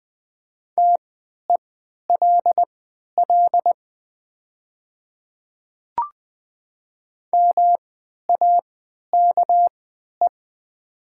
Morse code, heard as a keyed tone, about 20 wpm, spaced as for 11 wpm, 700 Hz. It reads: TELL MAKE
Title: 3x Character Spacing - Top 200 Words - 20wpm